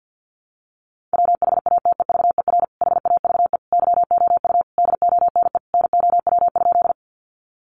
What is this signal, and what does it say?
Morse code, heard as a keyed tone, about 50 wpm, 700 Hz. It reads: WHATEVER HAVE YOU DONE NOW?
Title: Sentences from Top 1100 Words 50wpm